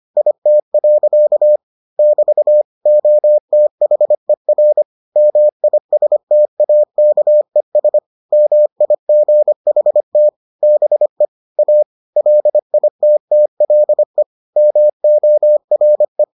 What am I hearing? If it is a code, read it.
Morse code, heard as a keyed tone, about 25 wpm, 600 Hz. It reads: IT. <BT> OTHER MISTAKES MIGHT BE A LITTLE MORE